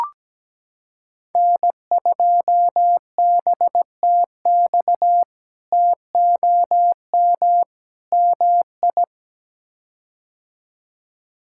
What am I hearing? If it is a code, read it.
Morse code, heard as a keyed tone, about 17 wpm, 700 Hz. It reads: N2BTX TOM MI
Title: K1USN SST QSO Practice - 17wpm